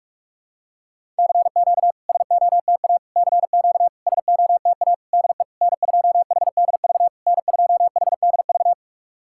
Morse code, heard as a keyed tone, about 46 wpm, 690 Hz. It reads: CQ SOTA CQ SOTA DE N2HDV N2HDV
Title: SOTA QSO Practice 45wpm